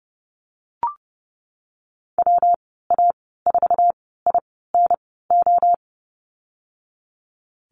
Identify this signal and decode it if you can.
Morse code, heard as a keyed tone, about 30 wpm, spaced as for 17 wpm, 700 Hz. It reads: WA4INO
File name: Lesson 19g - Letter w - Call Signs 30wpm